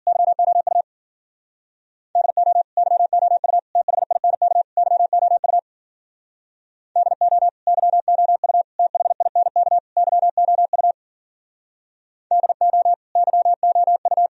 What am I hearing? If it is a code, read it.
Morse code, heard as a keyed tone, about 49 wpm, 700 Hz. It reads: YOU DO YOU THINK YOU DO YOU THINK YOU DO YOU
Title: Speed-Racing - Sets of 4 Words 40wpm